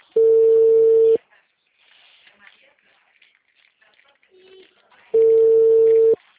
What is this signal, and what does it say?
Ringback tone (China, 450 Hz), with speech.
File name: out- -2022- - - .36179